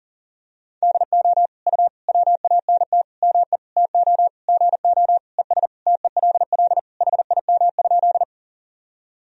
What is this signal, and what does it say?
Morse code, heard as a keyed tone, about 40 wpm, 700 Hz. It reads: DO U WANT ME TO GO ES TELL HIM?
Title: Sentences from Top 200 Words and Abbreviations 40wpm